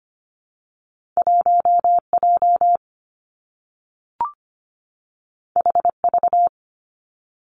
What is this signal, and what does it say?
Morse code, heard as a keyed tone, about 25 wpm, 700 Hz. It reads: 1 J H V
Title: Two Challenging Letter-Number Combinations 25wpm